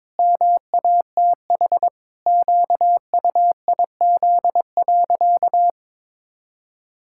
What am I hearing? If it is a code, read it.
Morse code, heard as a keyed tone, about 22 wpm, 700 Hz. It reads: MATH QUIZ.
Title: Pangram Challenge v2 - 22wpm